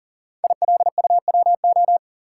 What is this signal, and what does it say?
Morse code, heard as a keyed tone, about 41 wpm, 700 Hz. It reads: ILUWO
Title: Code Groups - 5 Letters - 40wpm